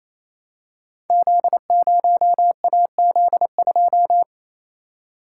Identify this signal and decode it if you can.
Morse code, heard as a keyed tone, about 27 wpm, 700 Hz. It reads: Z0AZ2
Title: Code Groups - 5 Letter:Numbers - 28wpm